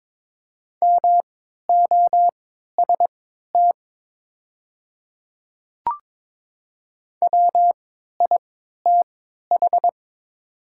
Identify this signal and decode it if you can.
Morse code, heard as a keyed tone, about 22 wpm, spaced as for 12 wpm, 700 Hz. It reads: MOST WITH
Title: 3x Character Spacing - Top 100 Words - 22wpm